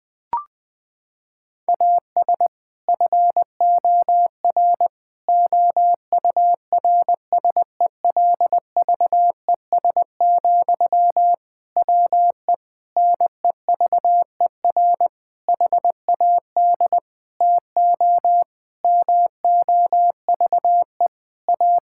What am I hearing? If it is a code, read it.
Morse code, heard as a keyed tone, about 20 wpm, 700 Hz. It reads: AS FOR OURSELVES, WE NEVER HAD TO MOVE A
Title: Sentences from Top 1600 Words 20wpm